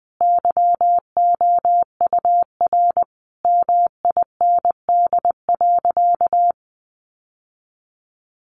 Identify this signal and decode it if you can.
Morse code, heard as a keyed tone, about 20 wpm, 700 Hz. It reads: YOUR MIND.
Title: Sentences from Top 1100 Words 20wpm